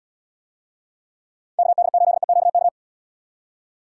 Morse code, heard as a keyed tone, about 76 wpm, 700 Hz. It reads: KD8EQK